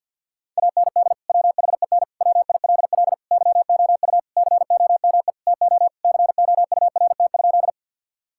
Morse code, heard as a keyed tone, about 50 wpm, 700 Hz. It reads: AND WHEN WILL YOU COME TO COURT?
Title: Sentences from Top 1600 Words 50wpm